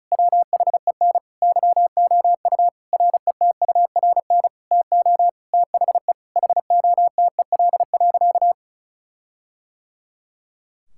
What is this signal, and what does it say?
Morse code, heard as a keyed tone, about 36 wpm, 700 Hz. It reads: WHEN YOU RETURN TO THE HOTEL.
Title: Sentences from Top 1600 Words 35wpm